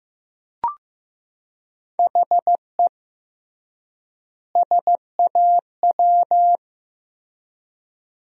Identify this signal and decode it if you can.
Morse code, heard as a keyed tone, about 15 wpm, 700 Hz. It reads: HE SAW
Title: 3x Word-Spacing - Sentences from Top 500 Words - 15wpm